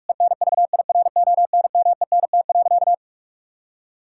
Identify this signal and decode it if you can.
Morse code, heard as a keyed tone, about 46 wpm, 700 Hz. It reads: ENVIRONMENT.
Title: Sentences from Top 1000 Words - Review Part 5 - 45wpm